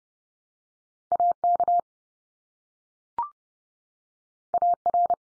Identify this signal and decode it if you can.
Morse code, heard as a keyed tone, about 30 wpm, 700 Hz. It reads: AK AR
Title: US States 30WPM